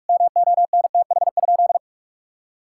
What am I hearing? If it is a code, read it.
Morse code, heard as a keyed tone, about 41 wpm, 700 Hz. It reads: MONTH?
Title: Sentences from Top 1000 Words - Review Part 6 - 45wpm